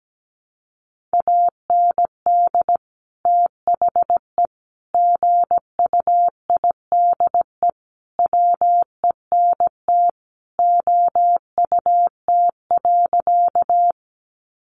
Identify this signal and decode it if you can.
Morse code, heard as a keyed tone, about 17 wpm, 700 Hz. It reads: AND THE GUIDE WENT OUT.
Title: Sentences from Top 600 Words 17wpm